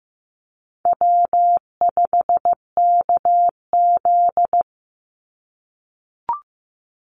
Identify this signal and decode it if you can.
Morse code, heard as a keyed tone, about 15 wpm, 700 Hz. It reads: W5KZ
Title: USA Call Signs 15wpm